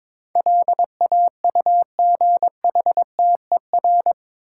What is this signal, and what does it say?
Morse code, heard as a keyed tone, about 22 wpm, 700 Hz. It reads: LAUGHTER
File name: practice / 22wpm / Eight Letter Words - 22wpm